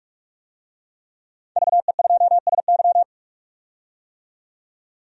Morse code, heard as a keyed tone, about 42 wpm, 700 Hz. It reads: VE2SY
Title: DXCC-100 Call Signs - 5 Characters - 45wpm